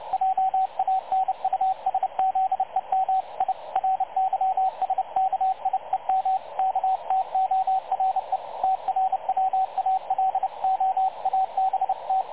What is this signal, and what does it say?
Morse code, heard as a keyed tone, about 30 wpm, 760 Hz, with radio noise. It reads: JANUSZEM I RYSKIEM KTORE TRWALO ABT
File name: dlazaawansowanych / nagraniazpasma / websdr recording 3567.0kHz